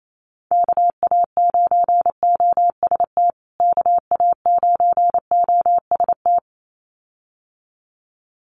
Morse code, heard as a keyed tone, about 28 wpm, 700 Hz. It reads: KA9OST KA9OST
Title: USA Call Signs Repeated 2x 28wpm